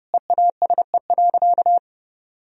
Morse code, heard as a keyed tone, about 30 wpm, 700 Hz. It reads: EASE.
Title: Sentences from Top 1000 Words - Review Part 1 - 30wpm